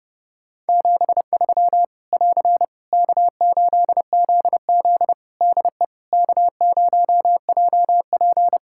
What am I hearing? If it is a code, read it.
Morse code, heard as a keyed tone, about 30 wpm, 700 Hz. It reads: 73 <AR> K8ZZ DE K0JP
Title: Practice QSO - Exchange 4 - Ending - 30wpm